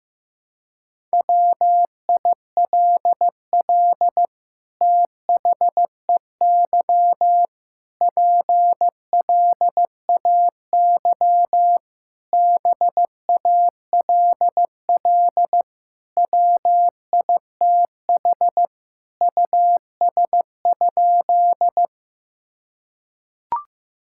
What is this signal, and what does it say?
Morse code, heard as a keyed tone, about 16 wpm, 700 Hz. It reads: WILL THEY PLAY BALL WITH US?
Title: Sentences from Top 1000 Words - Review Part 6 - 15wpm